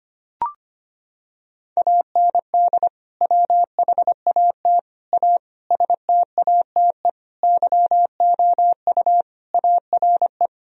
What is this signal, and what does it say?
Morse code, heard as a keyed tone, about 25 wpm, 700 Hz. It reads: AND WHAT A STATE YOU ARE
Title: Sentences from Top 1000 Words - Review Part 2 - 25wpm